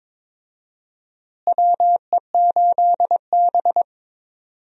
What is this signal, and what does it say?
Morse code, heard as a keyed tone, about 22 wpm, 700 Hz. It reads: WE8B
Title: USA Call Signs - 4 Characters - Repeated 2x - 22wpm